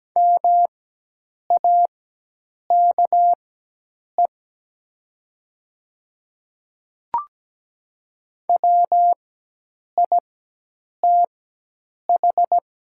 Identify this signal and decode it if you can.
Morse code, heard as a keyed tone, about 17 wpm, spaced as for 8 wpm, 700 Hz. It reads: MAKE WITH
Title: 4x Character Spacing - Top 100 Words - 17wpm